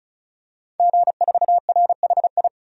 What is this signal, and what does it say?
Morse code, heard as a keyed tone, about 34 wpm, 700 Hz. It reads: G4RHI
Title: DXCC-100 Call Signs - 5 Characters - 35wpm